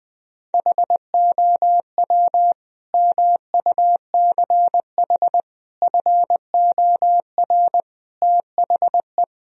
Morse code, heard as a keyed tone, about 20 wpm, 700 Hz. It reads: HOW MUCH FOR THE
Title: Sentences from Top 1000 Words - Review Part 3 - 20wpm